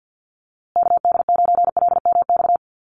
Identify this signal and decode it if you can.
Morse code, heard as a keyed tone, about 48 wpm, 700 Hz. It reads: KD9LMX
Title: USA Call Signs Repeated 3x 50wpm